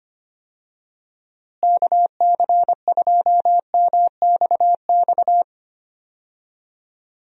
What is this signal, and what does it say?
Morse code, heard as a keyed tone, about 25 wpm, 700 Hz. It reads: KC2MXX
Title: USA Call Signs - 6 Characters - Repeated 3x - 25wpm